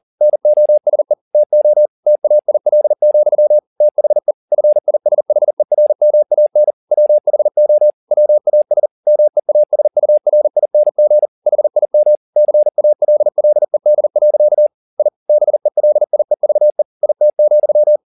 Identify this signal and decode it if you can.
Morse code, heard as a keyed tone, about 40 wpm, 600 Hz. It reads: NOSE TO TAIL, THE FISHERMAN WHO WAS MEASURING HIM CALLED. I BELIEVE IT,
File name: the old man and the sea 2 40wpm